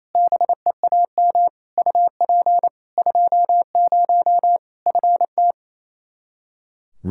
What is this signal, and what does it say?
Morse code, heard as a keyed tone, about 28 wpm, 700 Hz. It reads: BEAM UP 20 FT
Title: Practice QSO - Exchange 2 - Info - 28wpm